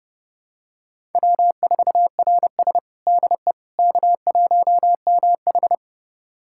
Morse code, heard as a keyed tone, about 30 wpm, 700 Hz. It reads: W4RS DE K1MH
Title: USA Callsign DE Callsign - 4 Characters - 30wpm